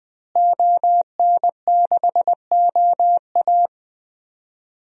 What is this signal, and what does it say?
Morse code, heard as a keyed tone, about 20 wpm, 700 Hz. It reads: ON6OA
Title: DXCC-100 Call Signs - 5 Characters - Repeated 3x - 20wpm